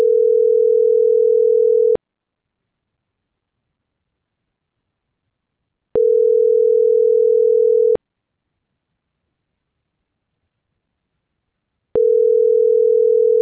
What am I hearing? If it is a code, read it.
Ringback tone (North America), 440 + 480 Hz.